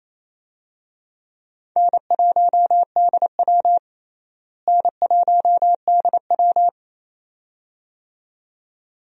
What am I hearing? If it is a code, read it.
Morse code, heard as a keyed tone, about 28 wpm, 700 Hz. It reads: N1DW N1DW
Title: USA Call Signs - 4 Characters - Repeated 2x - 28wpm